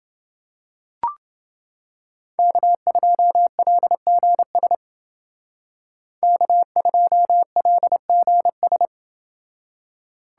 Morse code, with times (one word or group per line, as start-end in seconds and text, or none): 2.4-4.7 K2LGS
6.2-8.9 K2LGS